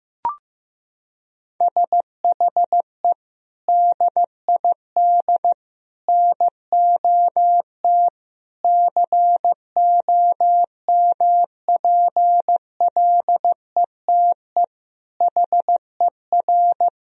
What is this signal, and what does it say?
Morse code, heard as a keyed tone, about 15 wpm, 700 Hz. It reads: SHE DID NOT COMPLETE HER
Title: Sentences from Top 1000 Words - Review Part 2 - 15wpm